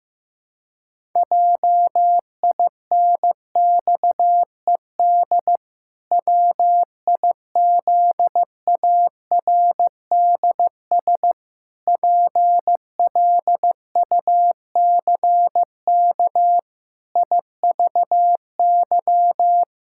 Morse code, heard as a keyed tone, about 15 wpm, 700 Hz. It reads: JINXED WIZARDS PLUCK IVY
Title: Pangram Challenge v2 - 15wpm